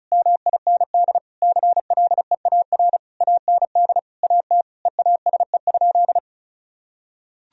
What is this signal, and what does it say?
Morse code, heard as a keyed tone, about 36 wpm, 700 Hz. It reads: MIND CLEAR AND AT EASE?
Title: Sentences from Top 1000 Words - Review Part 1 - 35wpm